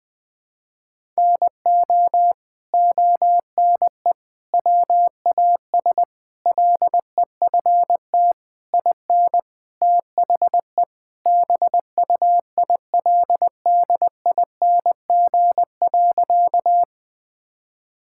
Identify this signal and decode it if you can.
Morse code, heard as a keyed tone, about 20 wpm, 700 Hz. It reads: NO ONE WAS LEFT IN THE BUILDING.
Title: Sentences from Top 1000 Words - Review Part 5 - 20wpm